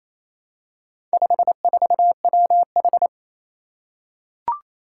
Morse code, heard as a keyed tone, about 28 wpm, 700 Hz. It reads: 54WH
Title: Four Challenging Letter-Number Combinations 28wpm